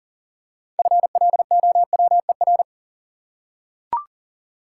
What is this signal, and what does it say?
Morse code, heard as a keyed tone, about 40 wpm, 700 Hz. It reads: FLOWER
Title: Six Letter Words - 40wpm